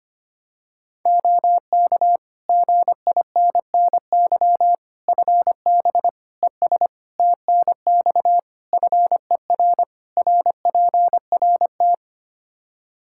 Morse code, heard as a keyed tone, about 25 wpm, 700 Hz. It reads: OK GINNY FB ES TNX FER RPRT
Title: Practice QSO - Exchange 2 - Info - 25wpm